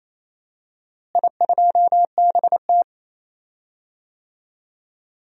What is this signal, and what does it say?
Morse code, heard as a keyed tone, about 27 wpm, 700 Hz. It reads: I2BT
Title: DXCC-100 Call Signs - 4 Characters - Repeated 3x - 28wpm